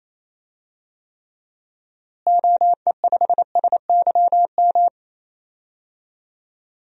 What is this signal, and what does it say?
Morse code, heard as a keyed tone, about 28 wpm, 700 Hz. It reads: OE5SYM
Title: DXCC-100 Call Signs - 6 Characters - Repeated 2x - 28wpm